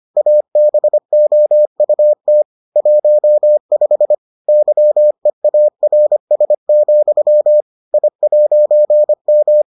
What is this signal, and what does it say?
Morse code, heard as a keyed tone, about 25 wpm, 600 Hz. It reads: ABOUT 15 YEARS, I'M